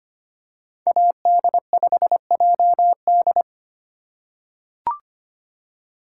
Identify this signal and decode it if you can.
Morse code, heard as a keyed tone, about 25 wpm, 700 Hz. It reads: AD5JD
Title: USA Call Signs - 5 Characters - 4x Character Spacing - 25wpm